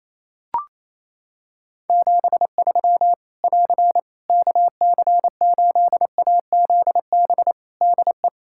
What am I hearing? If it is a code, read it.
Morse code, heard as a keyed tone, about 28 wpm, 700 Hz. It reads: 73 <AR> KC8AZB DE
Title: Practice QSO - Exchange 4 - Ending - 28wpm